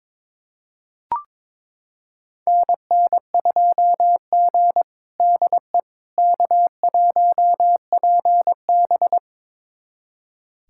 Morse code, heard as a keyed tone, about 22 wpm, 700 Hz. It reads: NN2G DE K1PB
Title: USA Callsign DE Callsign - 4 Characters - 22wpm